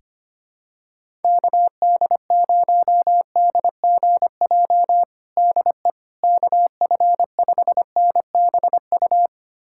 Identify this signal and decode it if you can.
Morse code, heard as a keyed tone, about 25 wpm, 700 Hz. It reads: KD0DGJ DE KF5NBU
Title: USA Callsign DE Callsign - 6 Characters - 25wpm